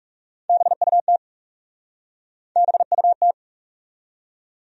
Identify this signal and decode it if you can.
Morse code, heard as a keyed tone, about 40 wpm, 700 Hz. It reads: BUT BUT